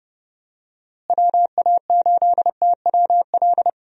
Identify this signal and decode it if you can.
Morse code, heard as a keyed tone, about 30 wpm, 700 Hz. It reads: WA8TWL
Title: USA Call Signs - 6 Characters - Repeated 3x - 30wpm